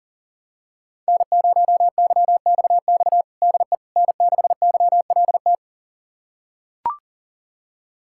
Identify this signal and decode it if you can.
Morse code, heard as a keyed tone, about 40 wpm, 700 Hz. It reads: N0YXX DE N6YLT
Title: USA Callsign DE Callsign - 5 Characters - 40wpm